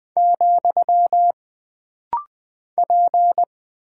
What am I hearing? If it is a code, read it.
Morse code, heard as a keyed tone, about 20 wpm, 700 Hz. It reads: , P
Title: Single Character (Letter:Number:Punctuation:Prosign) - Rapid-Fire - 20wpm